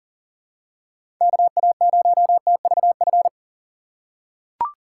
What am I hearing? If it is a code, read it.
Morse code, heard as a keyed tone, about 40 wpm, 700 Hz. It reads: KA0TVF
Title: USA Call Signs - 6 Characters - 4x Character Spacing - 40wpm